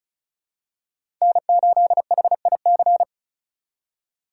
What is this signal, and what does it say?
Morse code, heard as a keyed tone, about 34 wpm, 700 Hz. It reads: N8HIC
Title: USA Call Signs - 5 Characters - 8x Character Spacing - 35wpm